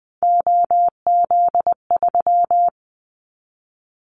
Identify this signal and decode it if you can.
Morse code, heard as a keyed tone, about 20 wpm, 700 Hz. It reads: OZ3
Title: Callsign Prefixes for Top 100 DXCC - 20wpm